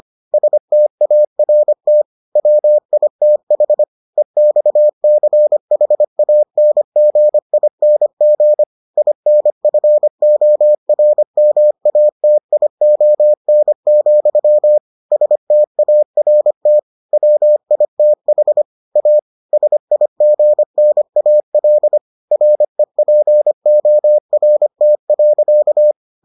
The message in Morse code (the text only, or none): START WITH EXCHANGING INFORMATION, START WITH A SIGNAL REPORT.